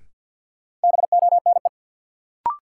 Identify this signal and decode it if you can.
Morse code, heard as a keyed tone, about 52 wpm, 700 Hz. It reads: BONE
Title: Sentences from Top 1600 Words - New Word Drill Alt - 50wpm